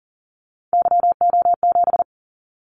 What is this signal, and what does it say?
Morse code, heard as a keyed tone, about 43 wpm, 700 Hz. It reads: YO7
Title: Callsign Prefixes for Top 25 DXCC - 40wpm